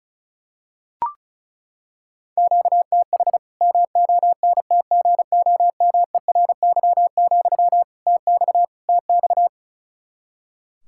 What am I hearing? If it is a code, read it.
Morse code, heard as a keyed tone, about 36 wpm, 700 Hz. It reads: QTH MONTGOMERY, TX TX
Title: K1USN SST QSO Practice 35wpm